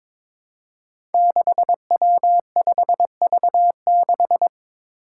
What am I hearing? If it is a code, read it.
Morse code, heard as a keyed tone, about 22 wpm, 700 Hz. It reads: 6W5V6
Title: Five Challenging Letter-Number Combinations 22wpm